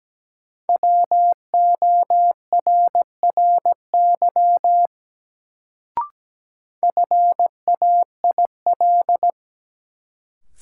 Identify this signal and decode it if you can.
Morse code, heard as a keyed tone, about 17 wpm, 700 Hz. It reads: WORRY FAIL
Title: Sentences from Top 1100 Words - New Word Drill - 17wpm